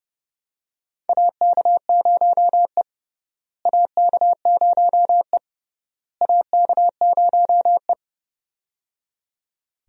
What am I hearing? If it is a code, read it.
Morse code, heard as a keyed tone, about 30 wpm, 700 Hz. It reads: AK0E AK0E AK0E
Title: USA Call Signs - 4 Characters - Repeated 3x - 30wpm